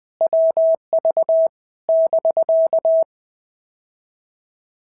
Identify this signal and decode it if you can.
Morse code, heard as a keyed tone, about 20 wpm, 650 Hz. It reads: WV <BK>
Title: POTA QSO Practice - 20wpm